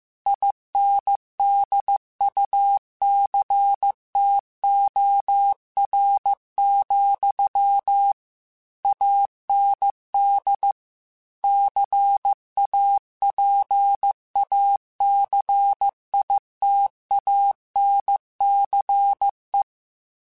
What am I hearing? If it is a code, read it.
Morse code, heard as a keyed tone, about 15 wpm, 800 Hz. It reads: INDUCTOR, AND CAPACITANCE
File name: Technical Subjects / Technical Subjects-mp3s / Reactance @13